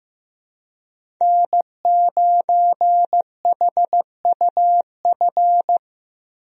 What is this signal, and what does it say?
Morse code, heard as a keyed tone, about 15 wpm, 700 Hz. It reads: N9HUF
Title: USA Call Signs - 5 Characters - 15wpm